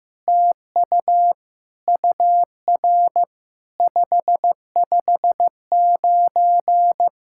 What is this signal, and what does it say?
Morse code, heard as a keyed tone, about 15 wpm, 700 Hz. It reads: TU UR 559
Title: SOTA QSO Practice 15wpm